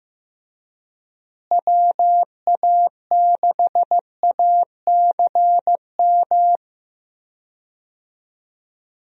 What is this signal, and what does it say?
Morse code, heard as a keyed tone, about 15 wpm, 700 Hz. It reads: WA6ACM
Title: USA Call Signs - 6 Characters - Repeated 3x - 15wpm